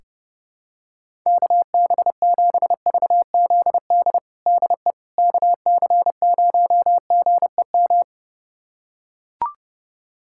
Morse code, heard as a keyed tone, about 30 wpm, 700 Hz. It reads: KB7VZD DE KC0GEM
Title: USA Callsign DE Callsign 30wpm